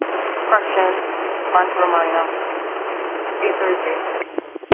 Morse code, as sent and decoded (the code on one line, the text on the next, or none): none